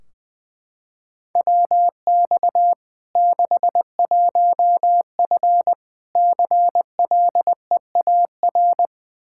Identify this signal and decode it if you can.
Morse code, heard as a keyed tone, about 20 wpm, 700 Hz. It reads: WX 61F CLEAR
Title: K1USN SST QSO Practice 20wpm